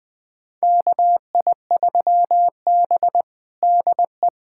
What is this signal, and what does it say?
Morse code, heard as a keyed tone, about 21 wpm, 700 Hz. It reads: KI3B DE
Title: USA Callsign DE Callsign 20wpm